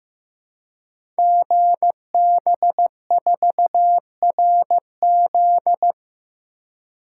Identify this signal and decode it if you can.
Morse code, heard as a keyed tone, about 15 wpm, 700 Hz. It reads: GB4RZ
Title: Code Groups - 5 Letter:Numbers - 15wpm